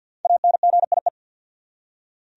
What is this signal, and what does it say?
Morse code, heard as a keyed tone, about 48 wpm, 700 Hz. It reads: ANGIE
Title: USA Common Names - Max 5 Letters - 50wpm